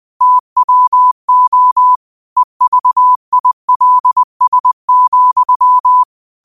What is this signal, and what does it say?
Morse code, heard as a keyed tone, about 20 wpm, 1000 Hz. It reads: TWO EVILS,